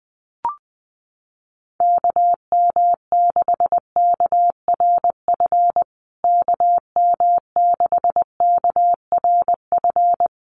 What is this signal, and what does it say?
Morse code, heard as a keyed tone, about 20 wpm, 700 Hz. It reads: KM6KRF KM6KRF
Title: USA Call Signs Repeated 3x 20wpm